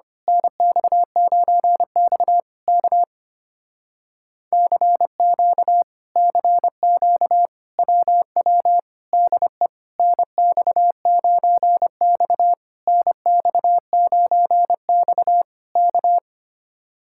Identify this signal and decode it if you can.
Morse code, heard as a keyed tone, about 25 wpm, 700 Hz. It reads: NX9X K CQ CQ WW DE NX9X NX9X K
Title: Speed-Racing Common QSO Phrases - 25wpm